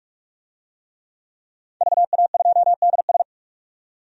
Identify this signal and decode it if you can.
Morse code, heard as a keyed tone, about 45 wpm, 700 Hz. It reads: VA2DS
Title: DXCC-100 Call Signs - 5 Characters - 45wpm